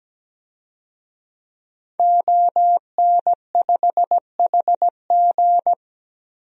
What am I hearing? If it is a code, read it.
Morse code, heard as a keyed tone, about 17 wpm, 700 Hz. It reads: ON5HG
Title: DXCC-100 Call Signs - 5 Characters - Repeated 3x - 17wpm